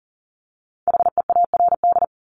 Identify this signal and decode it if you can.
Morse code, heard as a keyed tone, about 40 wpm, 700 Hz. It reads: HEARD
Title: Multi-letter Phonemes Repeated 3x 40wpm